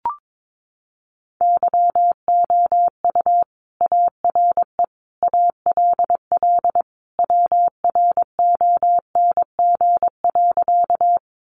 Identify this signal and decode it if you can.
Morse code, heard as a keyed tone, about 22 wpm, 700 Hz. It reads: YOU ARE ALL WRONG.
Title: Sentences from Top 900 Words 22wpm